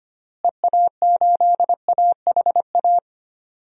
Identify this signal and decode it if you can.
Morse code, heard as a keyed tone, about 25 wpm, 700 Hz. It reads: EA8AHA
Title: DXCC-100 Call Signs - 6 Characters - Repeated 2x - 25wpm